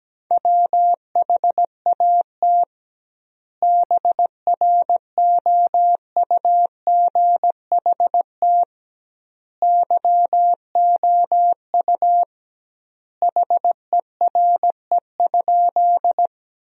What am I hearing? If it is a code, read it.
Morse code, heard as a keyed tone, about 17 wpm, 700 Hz. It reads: WHAT BROUGHT YOU HERE?
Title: 2x Word-Spacing - Sentences from Top 500 Words - 17wpm